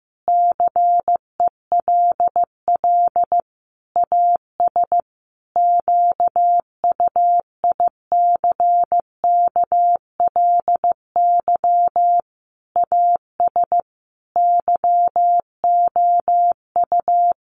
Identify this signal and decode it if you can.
Morse code, heard as a keyed tone, about 15 wpm, 700 Hz. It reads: CELL AS QUICKLY AS YOU
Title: Sentences from Top 1100 Words 15wpm